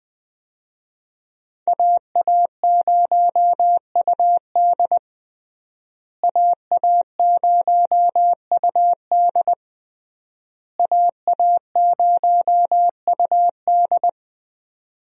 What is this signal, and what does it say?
Morse code, heard as a keyed tone, about 20 wpm, 700 Hz. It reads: AA0UD AA0UD AA0UD
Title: USA Call Signs - 5 Characters - Repeated 3x - 20wpm